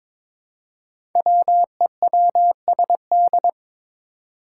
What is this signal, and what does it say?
Morse code, heard as a keyed tone, about 22 wpm, 700 Hz. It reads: WEWSD
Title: Five Challenging Letter-Number Combinations 22wpm